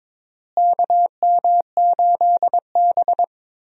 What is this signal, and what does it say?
Morse code, heard as a keyed tone, about 22 wpm, 700 Hz. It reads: KM8B
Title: USA Call Signs - 4 Characters - Repeated 2x - 22wpm